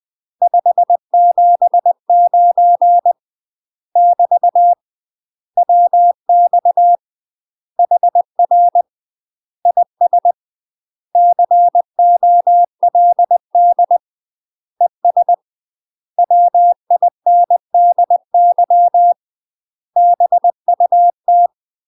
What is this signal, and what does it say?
Morse code, heard as a keyed tone, about 20 wpm, 700 Hz. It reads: 579 <BT> WX HR IS COLD ES WINDY BUT
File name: UK MASTER QSOs TRAINING / 20WPM / UK MASTER QSOs 0058